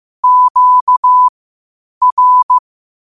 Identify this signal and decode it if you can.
Morse code, heard as a keyed tone, about 15 wpm, spaced as for 8 wpm, 1000 Hz. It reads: QR
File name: qotdfiles / 8wpm